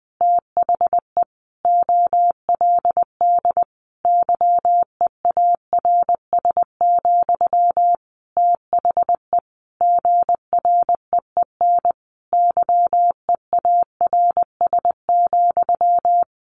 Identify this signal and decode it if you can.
Morse code, heard as a keyed tone, about 20 wpm, 700 Hz. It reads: THE OLD YEARS, THE GREEN YEARS,
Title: A Sound Of Thunder 20wpm